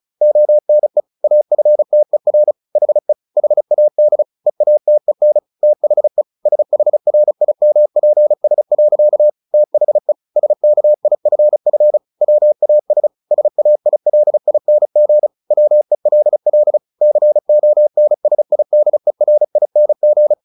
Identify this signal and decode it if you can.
Morse code, heard as a keyed tone, about 34 wpm, 600 Hz. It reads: ONE AFTER HE HAD EATEN THE SHRIMPS. THE SKIFF WAS SAILING WELL CONSIDERING
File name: the old man and the sea 2 35wpm